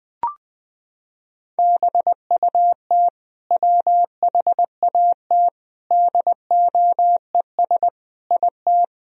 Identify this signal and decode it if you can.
Morse code, heard as a keyed tone, about 21 wpm, 700 Hz. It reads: BUT WHAT DOES IT
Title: Sentences from Top 1000 Words - Review Part 4 - 20wpm